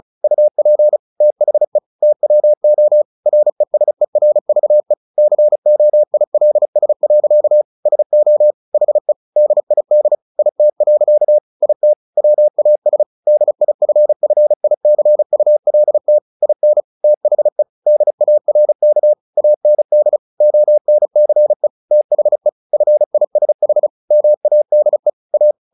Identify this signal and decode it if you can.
Morse code, heard as a keyed tone, about 34 wpm, 600 Hz. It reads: UP THE TWO RESERVE COILS. SO HE DID IT. IT WAS DIFFICULT IN THE DARK AND ONCE THE FISH MADE A
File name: the old man and the sea / the old man and the sea 1 35wpm